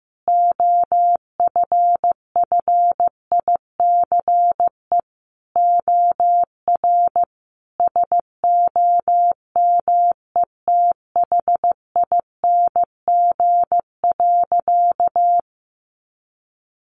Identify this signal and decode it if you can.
Morse code, heard as a keyed tone, about 15 wpm, 700 Hz. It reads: OFFICE OR SOMETHING.
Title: Sentences from Top 1000 Words 15wpm